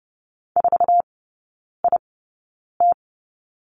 Morse code, heard as a keyed tone, about 30 wpm, spaced as for 9 wpm, 690 Hz. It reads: 4IT
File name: Lesson 17e - Letter u - Three Characters 30wpm